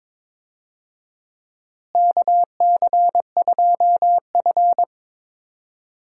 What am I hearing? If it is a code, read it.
Morse code, heard as a keyed tone, about 22 wpm, 700 Hz. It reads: KC2F
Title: USA Call Signs - 4 Characters - Repeated 2x - 22wpm